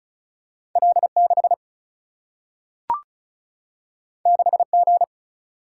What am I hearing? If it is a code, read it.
Morse code, heard as a keyed tone, about 36 wpm, 700 Hz. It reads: L 6 6 G
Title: Single-Letter Single-Number 35wpm